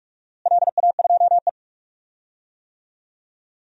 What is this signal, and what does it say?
Morse code, heard as a keyed tone, about 44 wpm, 700 Hz. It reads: LA2E
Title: DXCC-100 Call Signs - 4 Characters - Repeated 3x - 45wpm